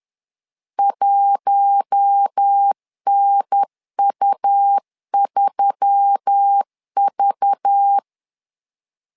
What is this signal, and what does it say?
Morse code, heard as a keyed tone, about 10 wpm, 800 Hz. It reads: 1NU3V
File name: LEZIONI / 5 LBVC / 5 LBV 10wpm 2248